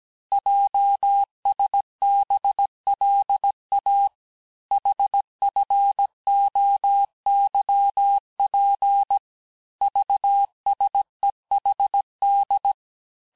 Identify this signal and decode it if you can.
Morse code, heard as a keyed tone, about 17 wpm, 800 Hz. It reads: JSBLA HFOYP VSEHD
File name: cours-du-lundi files / cours-1587-texte-chiffre-vitesse-16MPM